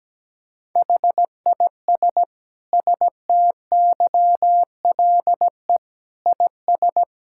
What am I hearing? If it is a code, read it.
Morse code, heard as a keyed tone, about 17 wpm, 700 Hz. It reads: HIS STYLE IS
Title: Sentences from Top 1000 Words - Review Part 5 - 17wpm